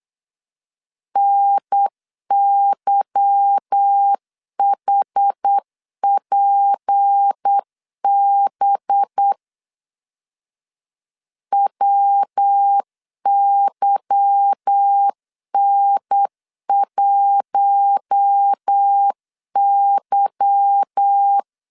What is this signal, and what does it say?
Morse code, heard as a keyed tone, about 8 wpm, 800 Hz. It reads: NYHPB WYN1Y
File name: LEZIONI / 8 YJ45 / 8 Y 8wpm 3245